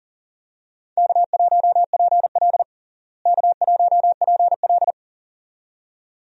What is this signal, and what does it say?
Morse code, heard as a keyed tone, about 40 wpm, 700 Hz. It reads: K1PL K1PL
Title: USA Call Signs - 4 Characters - Repeated 2x - 40wpm